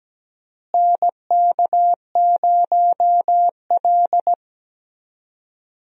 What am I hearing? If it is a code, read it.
Morse code, heard as a keyed tone, about 17 wpm, 700 Hz. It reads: NK0L
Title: USA Call Signs - 4 Characters - Repeated 3x - 17wpm